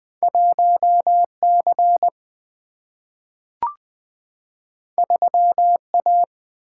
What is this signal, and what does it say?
Morse code, heard as a keyed tone, about 20 wpm, 700 Hz. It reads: 1 C 3 A
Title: Single-Letter Single-Number 20wpm